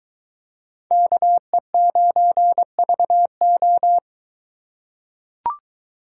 Morse code, heard as a keyed tone, about 23 wpm, 700 Hz. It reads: KE9VO